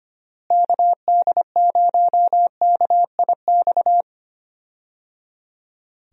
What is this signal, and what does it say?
Morse code, heard as a keyed tone, about 25 wpm, 700 Hz. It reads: KD0KIX
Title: USA Call Signs - 6 Characters - Repeated 3x - 25wpm